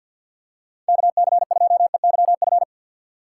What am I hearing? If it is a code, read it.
Morse code, heard as a keyed tone, about 51 wpm, 690 Hz. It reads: KC2EYF
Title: USA Call Signs - 4x Character Spacing - 50wpm